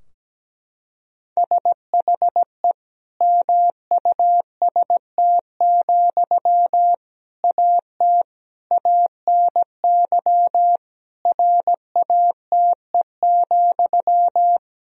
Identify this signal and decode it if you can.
Morse code, heard as a keyed tone, about 17 wpm, 700 Hz. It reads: SHE MUST, AT ANY RATE,
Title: Sentences from Top 1000 Words - Review Part 4 - 17wpm